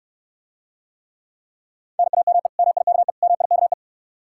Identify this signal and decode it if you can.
Morse code, heard as a keyed tone, about 69 wpm, 700 Hz. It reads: NAME MIKE MIKE